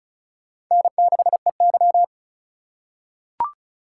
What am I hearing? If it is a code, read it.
Morse code, heard as a keyed tone, about 36 wpm, 700 Hz. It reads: N6EY
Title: USA Call Signs - 4x Character Spacing - 35wpm